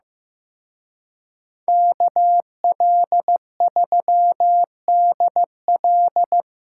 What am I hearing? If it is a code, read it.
Morse code, heard as a keyed tone, about 18 wpm, 700 Hz. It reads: KL3DL
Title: Speed Racing - USA Call Signs 15wpm